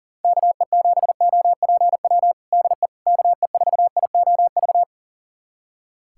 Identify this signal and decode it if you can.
Morse code, heard as a keyed tone, about 40 wpm, 700 Hz. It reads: KE7OPW DE KE4IOV
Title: USA Callsign DE Callsign - 6 Characters - 40wpm